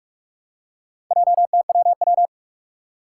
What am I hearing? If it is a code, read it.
Morse code, heard as a keyed tone, about 46 wpm, 700 Hz. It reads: JTWW
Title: Four Challenging Letter-Number Combinations 45wpm